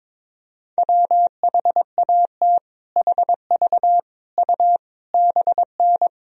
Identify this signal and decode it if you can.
Morse code, heard as a keyed tone, about 22 wpm, 700 Hz. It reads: WHAT HV U BN
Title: Sentences from Top 200 Words and Abbreviations 22wpm